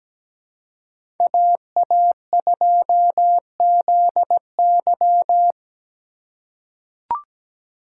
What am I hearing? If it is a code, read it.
Morse code, heard as a keyed tone, about 17 wpm, 700 Hz. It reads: AA2ZY
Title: USA Call Signs - 3x Character Spacing - 17wpm